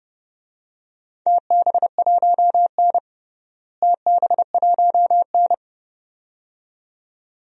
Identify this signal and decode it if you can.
Morse code, heard as a keyed tone, about 32 wpm, 700 Hz. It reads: TB1N TB1N
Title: DXCC-100 Call Signs - 4 Characters - Repeated 2x - 30wpm